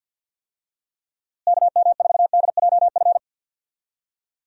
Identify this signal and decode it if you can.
Morse code, heard as a keyed tone, about 51 wpm, 700 Hz. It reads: KM4DJF
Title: USA Call Signs - 6 Characters - Repeated 2x - 50wpm